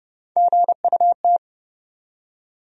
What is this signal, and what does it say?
Morse code, heard as a keyed tone, about 30 wpm, 700 Hz. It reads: GUT
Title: USA Call Sign Postfixes - 3 Letters 30wpm